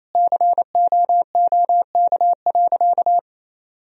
Morse code, heard as a keyed tone, about 28 wpm, 700 Hz. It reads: COOK.
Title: Sentences from Top 1000 Words - Review Part 4 - 28wpm